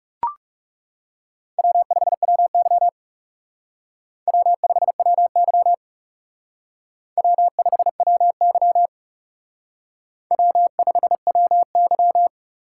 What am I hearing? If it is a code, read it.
Morse code, heard as a keyed tone, about 37 wpm, 700 Hz. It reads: W5WY W5WY W5WY W5WY
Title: Speed Racing - USA 1x2 Call Signs 30wpm